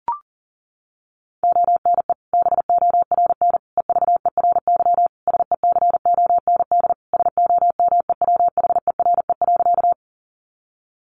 Morse code, heard as a keyed tone, about 40 wpm, 700 Hz. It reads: ONE BORN EVERY SECOND SOMEWHERE.
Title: Sentences from Top 1400 Words 40wpm